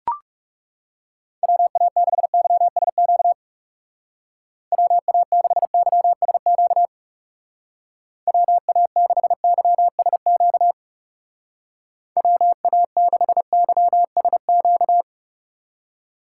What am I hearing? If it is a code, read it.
Morse code, heard as a keyed tone, about 38 wpm, 700 Hz. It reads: WA6YSQ WA6YSQ WA6YSQ WA6YSQ